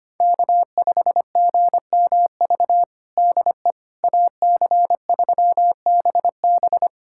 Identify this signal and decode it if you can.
Morse code, heard as a keyed tone, about 25 wpm, 700 Hz. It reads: K5GMV DE AC3BB
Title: USA Callsign DE Callsign - 5 Characters - 25wpm